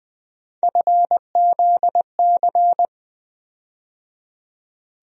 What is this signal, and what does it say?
Morse code, heard as a keyed tone, about 20 wpm, 700 Hz. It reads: FZC
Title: USA Call Sign Postfixes - 3 Letters 20wpm